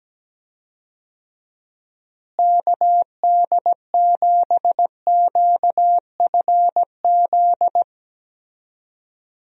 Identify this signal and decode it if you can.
Morse code, heard as a keyed tone, about 17 wpm, 700 Hz. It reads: KD7QFZ
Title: USA Call Signs - 6 Characters - Repeated 3x - 17wpm